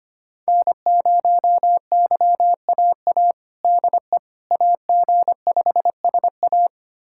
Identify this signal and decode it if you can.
Morse code, heard as a keyed tone, about 25 wpm, 700 Hz. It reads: N0YAA DE AG5SA
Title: USA Callsign DE Callsign 25wpm